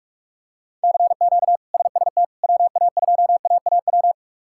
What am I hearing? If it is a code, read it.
Morse code, heard as a keyed tone, about 42 wpm, 700 Hz. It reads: CQ SST WA2AAW
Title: K1USN SST QSO Practice - 45wpm